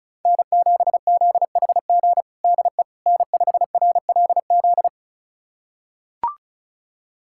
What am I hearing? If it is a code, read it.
Morse code, heard as a keyed tone, about 34 wpm, 700 Hz. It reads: N7ZHG DE N5RLZ
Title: USA Callsign DE Callsign - 5 Characters - 35wpm